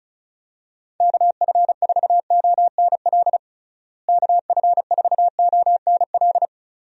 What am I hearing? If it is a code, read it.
Morse code, heard as a keyed tone, about 36 wpm, 700 Hz. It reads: KF4ONL KF4ONL
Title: USA Call Signs - 6 Characters - Repeated 2x - 35wpm